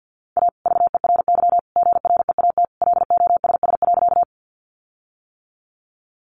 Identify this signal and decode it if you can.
Morse code, heard as a keyed tone, about 50 wpm, 700 Hz. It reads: A VERY GREAT LOSS.
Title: Sentences from Top 1400 Words 50wpm